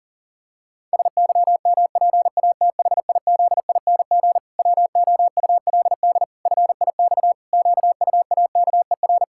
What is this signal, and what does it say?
Morse code, heard as a keyed tone, about 40 wpm, 700 Hz. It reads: SYMPATHIZING WOULD FIX QUAKER
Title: Pangram Challenge v2 - 40wpm